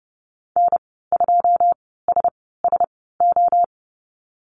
Morse code, heard as a keyed tone, about 30 wpm, spaced as for 17 wpm, 700 Hz. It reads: N2SSO
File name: Lesson 14g - Number 2 - Call Signs 30wpm